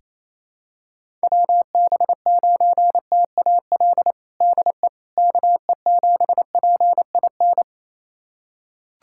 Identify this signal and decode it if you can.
Morse code, heard as a keyed tone, about 27 wpm, 700 Hz. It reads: WB9TAL DE KE7PIN
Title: USA Callsign DE Callsign - 6 Characters - 28wpm